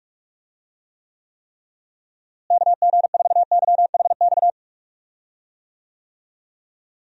Morse code, heard as a keyed tone, about 42 wpm, 700 Hz. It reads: KG4YHX
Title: USA Call Signs - 6 Characters - Repeated 3x - 45wpm